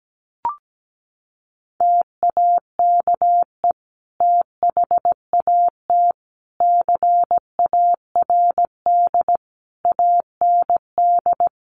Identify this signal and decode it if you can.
Morse code, heard as a keyed tone, about 17 wpm, 700 Hz. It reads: TAKE THAT CARD AND
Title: Sentences from Top 1400 Words 17wpm